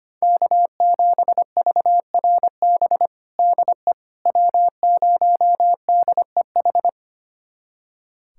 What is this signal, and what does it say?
Morse code, heard as a keyed tone, about 25 wpm, 700 Hz. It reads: K7VRB DE W0DEH
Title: USA Callsign DE Callsign - 5 Characters - 25wpm